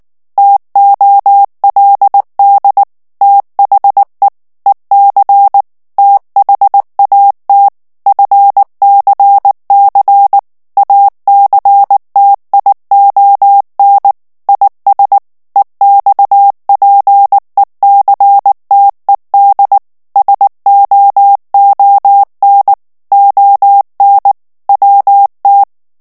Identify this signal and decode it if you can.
Morse code, heard as a keyed tone, about 19 wpm, 800 Hz. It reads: TOLD THE EC THAT FCC ACTION IS EXPECTED SOON ON WT
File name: ANGLAIS texte clair 3 vitesse 18 mots minute